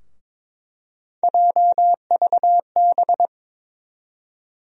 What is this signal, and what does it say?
Morse code, heard as a keyed tone, about 21 wpm, 700 Hz. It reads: JVB
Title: USA Call Sign Postfixes - 3 Letters 22wpm